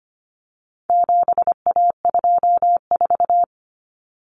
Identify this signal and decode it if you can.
Morse code, heard as a keyed tone, about 25 wpm, 700 Hz. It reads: 7A24
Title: Common Cut Numbers - 4-Digit 25wpm